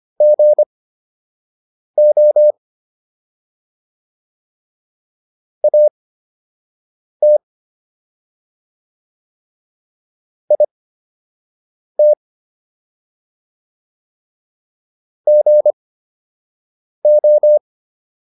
Morse code, heard as a keyed tone, about 25 wpm, spaced as for 6 wpm, 600 Hz. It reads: GO AT IT GO